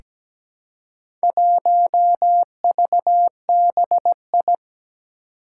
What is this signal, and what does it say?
Morse code, heard as a keyed tone, about 17 wpm, 700 Hz. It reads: 1VBI
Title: Four Challenging Letter-Number Combinations 17wpm